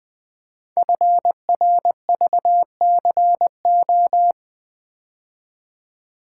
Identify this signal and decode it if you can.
Morse code, heard as a keyed tone, about 20 wpm, 700 Hz. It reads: FRVCO
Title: Code Groups - 5 Letters - 20wpm